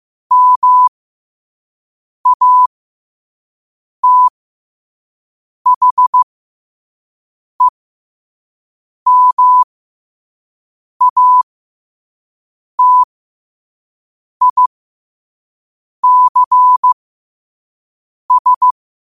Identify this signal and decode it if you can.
Morse code, heard as a keyed tone, about 15 wpm, spaced as for 5 wpm, 1000 Hz. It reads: MATHEMATICS
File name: qotdfiles / 5wpm